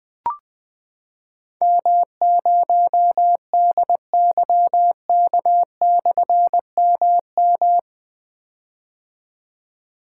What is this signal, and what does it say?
Morse code, heard as a keyed tone, about 21 wpm, 700 Hz. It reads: M0DYK/MM
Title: Difficult Call Signs 20wpm